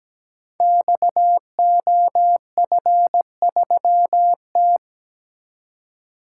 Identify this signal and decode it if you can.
Morse code, heard as a keyed tone, about 17 wpm, 700 Hz. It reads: XOF3T
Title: Code Groups - 5 Letter:Numbers - 17wpm